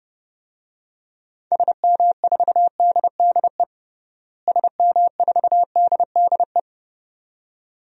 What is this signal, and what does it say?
Morse code, heard as a keyed tone, about 30 wpm, 700 Hz. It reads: SM4DDE SM4DDE
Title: DXCC-100 Call Signs - 6 Characters - Repeated 2x - 30wpm